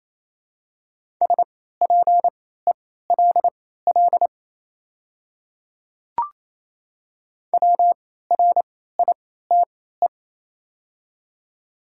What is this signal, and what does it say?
Morse code, heard as a keyed tone, about 27 wpm, spaced as for 16 wpm, 700 Hz. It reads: SPELL WRITE
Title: 3x Character Spacing - Top 300 Words - 28wpm